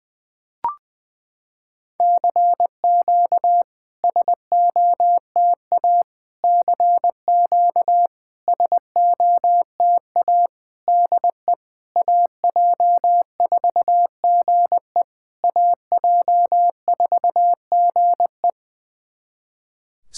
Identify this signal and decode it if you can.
Morse code, heard as a keyed tone, about 20 wpm, 700 Hz. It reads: CQ SOTA CQ SOTA DE AJ4GE AJ4GE
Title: SOTA QSO Practice 20wpm